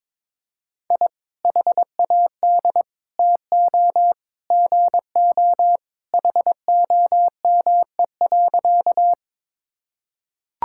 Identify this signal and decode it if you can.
Morse code, heard as a keyed tone, about 22 wpm, 700 Hz. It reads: I HAD TO GO HOME.
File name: practice / 22wpm / Sentences with 3-Word Phrases - 22wpm